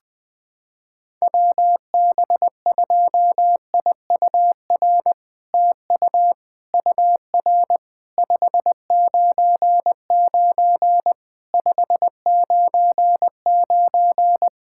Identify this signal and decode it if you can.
Morse code, heard as a keyed tone, about 20 wpm, 700 Hz. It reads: WB2IUR TU UR 599 599
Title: SOTA QSO Practice 20wpm